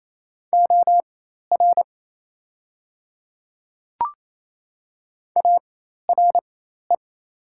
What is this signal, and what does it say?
Morse code, heard as a keyed tone, about 28 wpm, spaced as for 13 wpm, 690 Hz. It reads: OR ARE
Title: 4x Character Spacing - Top 100 Words - 28wpm